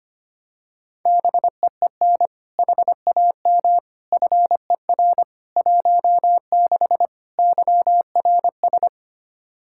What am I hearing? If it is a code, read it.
Morse code, heard as a keyed tone, about 25 wpm, 700 Hz. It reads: BEEN HAM FER 16 YRS
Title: Practice QSO - Exchange 3 - More Info - 25wpm